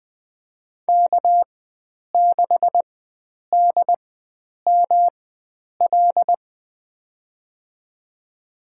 Morse code, heard as a keyed tone, about 20 wpm, spaced as for 9 wpm, 700 Hz. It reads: K6DML
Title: USA Call Signs - 4x Character Spacing - 20wpm